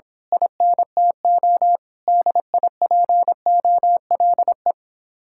Morse code, heard as a keyed tone, about 26 wpm, 700 Hz. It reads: INTO DIPOLE
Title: Speed-Racing Common QSO Phrases - 22wpm